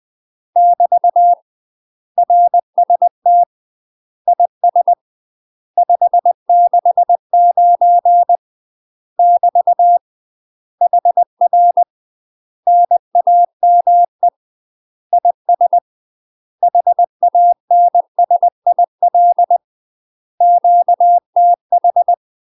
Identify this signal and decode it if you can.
Morse code, heard as a keyed tone, about 20 wpm, 700 Hz. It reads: <BT> RST IS 569 <BT> HR NAME IS HANSIL QTH
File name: UK MASTER QSOs TRAINING / 20WPM / UK MASTER QSOs 0018